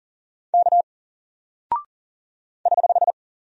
Morse code, heard as a keyed tone, about 40 wpm, 700 Hz. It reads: K <HH>
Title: Single Character (Letter:Number:Punctuation:Prosign) 40wpm